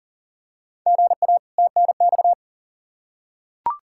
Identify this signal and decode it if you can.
Morse code, heard as a keyed tone, about 40 wpm, 700 Hz. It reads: GA TNX
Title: Practice QSO - Exchange 1 - Beginning - 40wpm